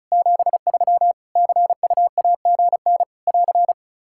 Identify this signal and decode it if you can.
Morse code, heard as a keyed tone, about 34 wpm, 700 Hz. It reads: 73 CUAGN <AR>
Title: K1USN SST QSO Practice 35wpm